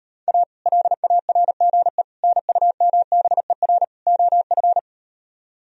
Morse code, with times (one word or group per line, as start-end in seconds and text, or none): 0.3-0.4 A
0.7-2.0 LARGE
2.2-3.8 NUMBER
4.1-4.8 OF